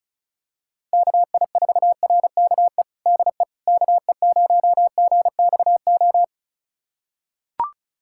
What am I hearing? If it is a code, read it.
Morse code, heard as a keyed tone, about 35 wpm, 700 Hz. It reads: KI4RKE DE KE0GXO
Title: USA Callsign DE Callsign - 6 Characters - 35wpm